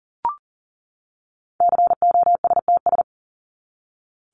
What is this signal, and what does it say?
Morse code, heard as a keyed tone, about 40 wpm, 700 Hz. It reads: COSTS
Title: ES:S Suffix - Top 100 Words with Base Words 40wpm